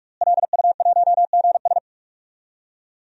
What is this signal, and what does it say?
Morse code, heard as a keyed tone, about 46 wpm, 700 Hz. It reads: LU1GS
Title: DXCC-100 Call Signs - 5 Characters - Repeated 3x - 45wpm